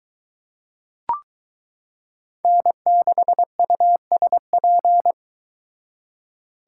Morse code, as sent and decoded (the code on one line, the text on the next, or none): -. -.... ..- ... .--.
N6USP